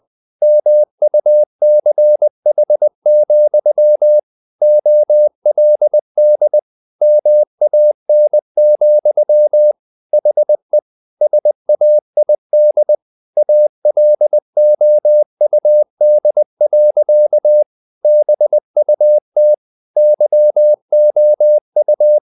Morse code, heard as a keyed tone, about 20 wpm, 600 Hz. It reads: MUCH, OLD MAN, HE SAID ALOUD. BUT YOU
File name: the old man and the sea part 2 20wpm